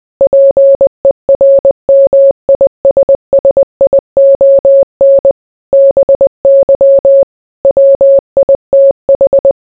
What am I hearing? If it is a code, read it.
Morse code, heard as a keyed tone, about 20 wpm, 550 Hz. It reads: PERMISSION BY WITH